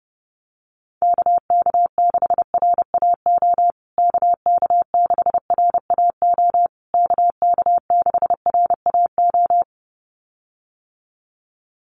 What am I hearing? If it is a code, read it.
Morse code, heard as a keyed tone, about 30 wpm, 700 Hz. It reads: KK6RAO KK6RAO KK6RAO
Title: USA Call Signs Repeated 3x 30wpm